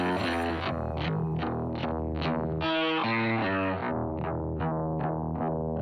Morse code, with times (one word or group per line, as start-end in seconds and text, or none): none